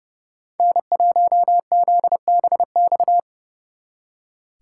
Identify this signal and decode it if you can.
Morse code, heard as a keyed tone, about 37 wpm, 700 Hz. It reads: N1ZBX
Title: Speed Racing - USA 1x3 Call Signs 30wpm